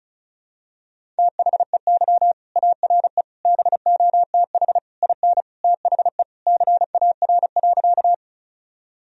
Morse code, heard as a keyed tone, about 36 wpm, 700 Hz. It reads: THEY ARE BOTH IN THE CAR.
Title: Sentences from Top 1000 Words - Review Part 1 - 35wpm